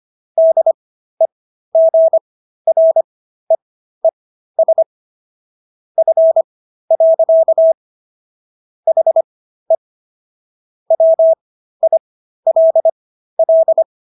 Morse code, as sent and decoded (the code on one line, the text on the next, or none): -.. --. .-. . ... / ..-. .-.-.- / .... / .-- .. .-.. .-..
DEGREES F. HE WILL